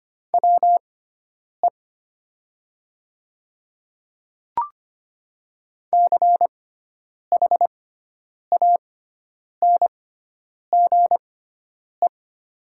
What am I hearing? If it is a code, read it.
Morse code, heard as a keyed tone, about 25 wpm, spaced as for 9 wpm, 700 Hz. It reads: WE CHANGE
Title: 6x Character Spacing - Top 200 Words - 25wpm